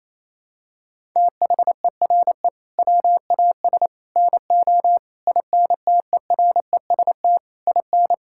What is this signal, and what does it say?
Morse code, heard as a keyed tone, about 28 wpm, 700 Hz. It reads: THERE WAS NO INTEREST IN
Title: Sentences from Top 1000 Words - Review Part 5 - 28wpm